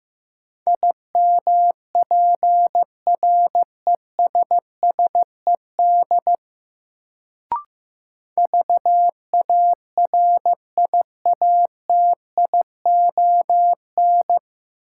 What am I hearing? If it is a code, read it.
Morse code, heard as a keyed tone, about 15 wpm, 700 Hz. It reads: IMPRESSED VARIATION
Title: Sentences from Top 1600 Words - New Word Drill - 15wpm